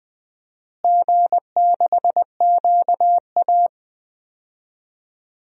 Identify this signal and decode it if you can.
Morse code, heard as a keyed tone, about 20 wpm, 700 Hz. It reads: G6QA
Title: DXCC-100 Call Signs - 4 Characters - Repeated 2x - 20wpm